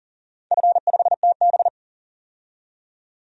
Morse code, heard as a keyed tone, about 42 wpm, 700 Hz. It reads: F5TB
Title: DXCC-100 Call Signs - 4 Characters - 40wpm